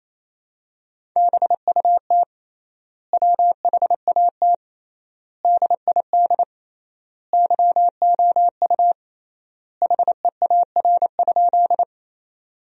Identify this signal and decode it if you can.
Morse code, heard as a keyed tone, about 28 wpm, 700 Hz. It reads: BUT WHAT DID YOU HEAR?
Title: 3x Word-Spacing - Sentences from Top 300 Words - 28wpm